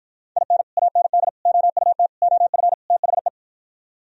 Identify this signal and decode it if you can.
Morse code, heard as a keyed tone, about 53 wpm, 700 Hz. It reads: IN AND OUT OF THE